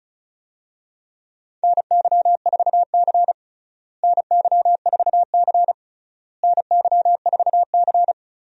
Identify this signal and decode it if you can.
Morse code, heard as a keyed tone, about 36 wpm, 700 Hz. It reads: NY4C NY4C NY4C
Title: USA Call Signs - 4 Characters - Repeated 3x - 35wpm